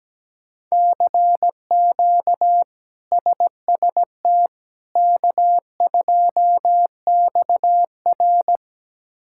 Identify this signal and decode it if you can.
Morse code, heard as a keyed tone, about 17 wpm, 700 Hz. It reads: CQ SST K2XR
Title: K1USN SST QSO Practice - 17wpm